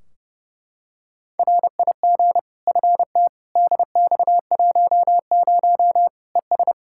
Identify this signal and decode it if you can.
Morse code, heard as a keyed tone, about 30 wpm, 700 Hz. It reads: RIG FT DX10 ES
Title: Practice QSO - Exchange 2 - Info - 30wpm